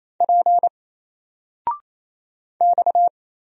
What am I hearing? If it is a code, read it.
Morse code, heard as a keyed tone, about 27 wpm, 700 Hz. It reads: P X
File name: Single-Letter-Number-28wpm